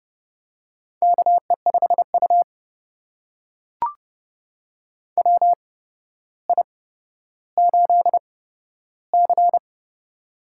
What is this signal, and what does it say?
Morse code, heard as a keyed tone, about 29 wpm, spaced as for 13 wpm, 700 Hz. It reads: KE5U WI8C
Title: USA Call Signs - 4 Characters - 8x Character Spacing - 30wpm